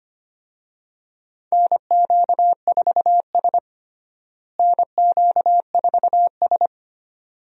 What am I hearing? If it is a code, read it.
Morse code, heard as a keyed tone, about 25 wpm, 700 Hz. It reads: NQ4S NQ4S
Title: USA Call Signs - 4 Characters - Repeated 2x - 25wpm